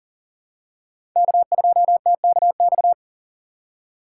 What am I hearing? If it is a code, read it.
Morse code, heard as a keyed tone, about 40 wpm, 700 Hz. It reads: K2TKX
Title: USA Call Signs - 5 Characters - 4x Character Spacing - 40wpm